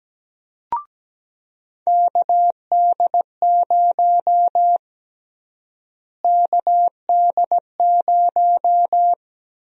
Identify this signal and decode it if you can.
Morse code, heard as a keyed tone, about 17 wpm, 700 Hz. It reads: KD0 KD0
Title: USA Call Sign Prefixes - Repeated 2x - 17wpm